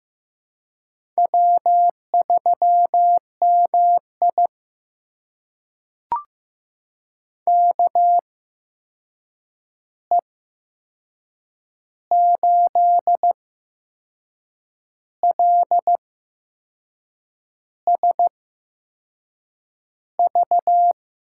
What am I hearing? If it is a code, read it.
Morse code, heard as a keyed tone, about 15 wpm, spaced as for 6 wpm, 700 Hz. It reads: W3MI KE8LSV
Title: USA Call Signs - 8x Character Spacing - 15wpm